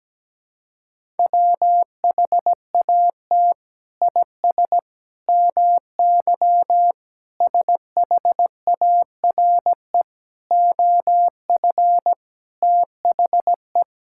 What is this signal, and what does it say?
Morse code, heard as a keyed tone, about 17 wpm, 700 Hz. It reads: WHAT IS MY SHARE OF THE
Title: Sentences from Top 1000 Words - Review Part 6 - 17wpm